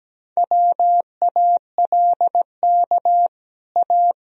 Morse code, heard as a keyed tone, about 17 wpm, 700 Hz. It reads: WALK A
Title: Sentences from Top 1000 Words - Review Part 1 - 17wpm